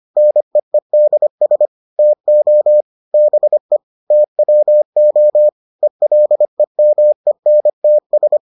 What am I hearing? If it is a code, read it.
Morse code, heard as a keyed tone, about 25 wpm, 600 Hz. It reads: NEEDS TO BE TWO ELEMENTS